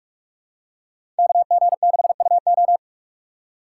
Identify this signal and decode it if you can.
Morse code, heard as a keyed tone, about 45 wpm, 700 Hz. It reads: KG6UO
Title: USA Call Signs - 3x Character Spacing - 45wpm